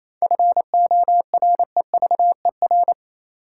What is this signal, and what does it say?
Morse code, heard as a keyed tone, about 28 wpm, 700 Hz. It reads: FOREVER
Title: Sentences from Top 1600 Words - New Word Drill - 28wpm